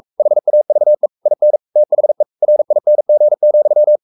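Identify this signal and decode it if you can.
Morse code, heard as a keyed tone, about 44 wpm, 600 Hz. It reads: HAVE IN THE RING,
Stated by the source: Morse code at 43 wpm sending Ken Follett's novel Never in English, from CW med Gnister.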